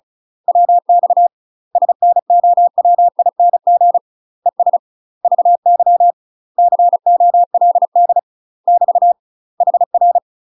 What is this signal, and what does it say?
Morse code, heard as a keyed tone, about 35 wpm, 700 Hz. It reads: WX SNOWING ES VY COLD <BT> HR
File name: UK MASTER QSOs TRAINING/35WPM